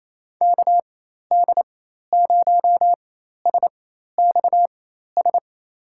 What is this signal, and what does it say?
Morse code, heard as a keyed tone, about 28 wpm, spaced as for 13 wpm, 700 Hz. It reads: KD0SXS
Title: USA Call Signs - 6 Characters - 4x Character Spacing - 28wpm